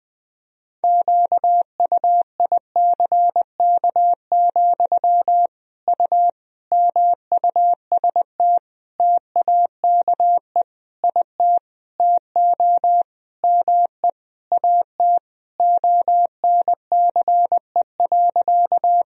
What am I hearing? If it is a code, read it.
Morse code, heard as a keyed tone, about 20 wpm, 700 Hz. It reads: QUICK, U MUST TAKE IT TO ME AT ONCE.
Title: Sentences from Top 500 Words and Abbreviations 20wpm